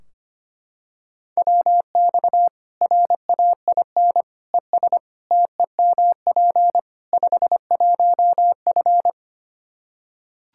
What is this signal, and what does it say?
Morse code, heard as a keyed tone, about 25 wpm, 700 Hz. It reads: WX RAIN ES TEMP 51F
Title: Practice QSO - Exchange 2 - Info - 25wpm